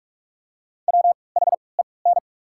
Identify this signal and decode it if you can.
Morse code, heard as a keyed tone, about 44 wpm, 690 Hz. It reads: W H E N
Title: 3x Character Spacing - Top 100 Words - 45wpm